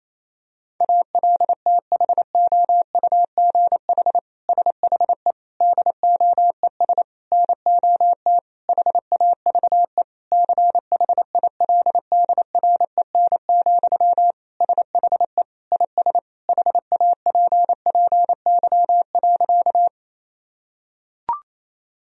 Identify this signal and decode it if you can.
Morse code, heard as a keyed tone, about 27 wpm, 700 Hz. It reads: ALTHOUGH SHE DOES NOT HAVE CHILDREN, SHE IS HAPPY.
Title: Sentences from Top 1000 Words - Review Part 3 - 28wpm